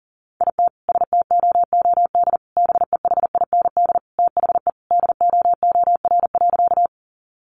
Morse code, heard as a keyed tone, about 40 wpm, 700 Hz. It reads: IT STOOD BEHIND THE DOOR.
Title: Sentences from Top 500 Words 40wpm